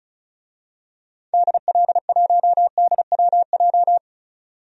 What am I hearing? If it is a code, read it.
Morse code, heard as a keyed tone, about 34 wpm, 700 Hz. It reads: DL1DWJ
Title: DXCC-100 Call Signs - 6 Characters - 35wpm